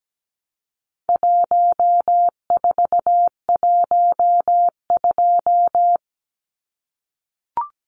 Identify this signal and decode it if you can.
Morse code, heard as a keyed tone, about 17 wpm, 700 Hz. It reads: 1412
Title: Numbers - 4-Digit 17wpm